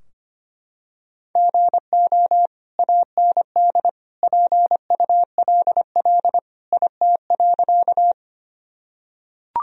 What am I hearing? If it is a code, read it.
Morse code, heard as a keyed tone, about 25 wpm, 700 Hz. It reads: GO AND PULL IT.
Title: Sentences from Top 1000 Words - Review Part 2 - 25wpm